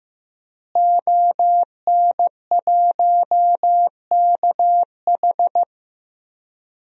Morse code, heard as a keyed tone, about 16 wpm, 700 Hz. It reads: ON1KH
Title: DXCC-100 Call Signs - 5 Characters - Repeated 3x - 15wpm